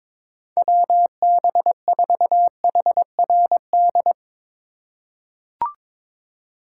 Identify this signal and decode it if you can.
Morse code, heard as a keyed tone, about 22 wpm, 700 Hz. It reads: WB4HRD
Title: USA Call Signs - 6 Characters - 3x Character Spacing - 22wpm